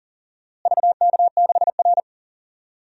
Morse code, heard as a keyed tone, about 38 wpm, 700 Hz. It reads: VK6R
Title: DXCC-100 Call Signs - 4 Characters - Repeated 2x - 40wpm